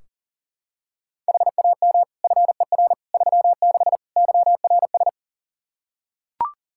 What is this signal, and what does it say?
Morse code, heard as a keyed tone, about 40 wpm, 700 Hz. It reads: HAM FER 36 YRS
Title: Practice QSO - Exchange 3 - More Info - 40wpm